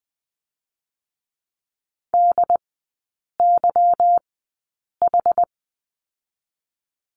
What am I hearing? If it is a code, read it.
Morse code, heard as a keyed tone, about 20 wpm, spaced as for 8 wpm, 700 Hz. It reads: DYH
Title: 184 Lesson 30e - Letter b - Three Characters 20wpm